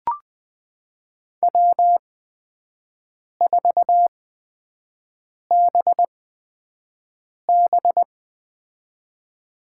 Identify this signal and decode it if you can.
Morse code, heard as a keyed tone, about 20 wpm, spaced as for 5 wpm, 700 Hz. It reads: W4BB
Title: USA Call Signs - 8x Character Spacing - 20wpm